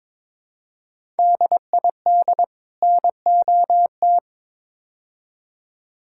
Morse code, heard as a keyed tone, about 22 wpm, 700 Hz. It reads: DID NOT
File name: practice / 22wpm / Build A Phrase - 5-Word Phrases - 22wpm